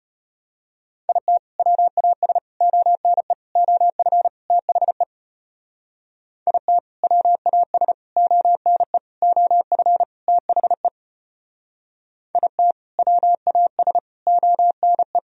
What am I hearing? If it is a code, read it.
Morse code, heard as a keyed tone, about 32 wpm, 700 Hz. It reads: IT WAS ONE OF THE IT WAS ONE OF THE IT WAS ONE